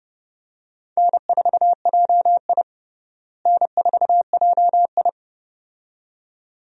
Morse code, heard as a keyed tone, about 30 wpm, 700 Hz. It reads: N4JI N4JI
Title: USA Call Signs - 4 Characters - Repeated 2x - 30wpm